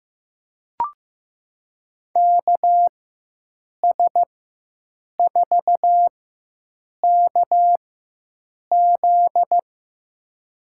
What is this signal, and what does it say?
Morse code, heard as a keyed tone, about 15 wpm, spaced as for 7 wpm, 700 Hz. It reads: KS4KZ
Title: USA Call Signs - 5 Characters - 4x Character Spacing - 15wpm